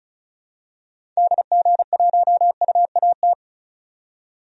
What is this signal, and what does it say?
Morse code, heard as a keyed tone, about 34 wpm, 700 Hz. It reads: DG1UAT
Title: DXCC-100 Call Signs - 6 Characters - 35wpm